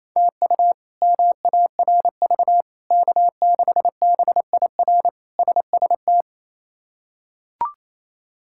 Morse code, heard as a keyed tone, about 28 wpm, 700 Hz. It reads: TU MARV K6BIR SST
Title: K1USN SST QSO Practice - 28wpm